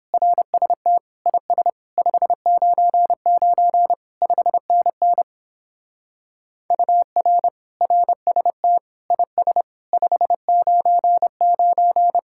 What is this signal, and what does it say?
Morse code, heard as a keyed tone, about 29 wpm, 700 Hz. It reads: RST IS 599 5NN UR RST IS 599